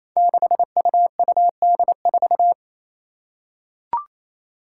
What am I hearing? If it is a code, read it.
Morse code, heard as a keyed tone, about 27 wpm, 700 Hz. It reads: 6UUD4
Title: Five Challenging Letter-Number Combinations 28wpm